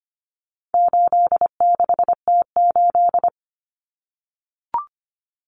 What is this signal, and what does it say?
Morse code, heard as a keyed tone, about 25 wpm, 700 Hz. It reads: 86T8
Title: Common Cut Numbers - 4-Digit 25wpm